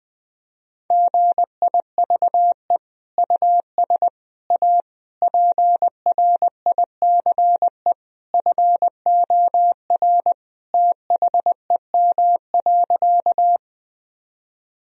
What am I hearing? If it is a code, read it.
Morse code, heard as a keyed tone, about 21 wpm, 700 Hz. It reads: GIVE US A PRICE FOR THEM.
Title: Sentences from Top 1000 Words - Review Part 4 - 20wpm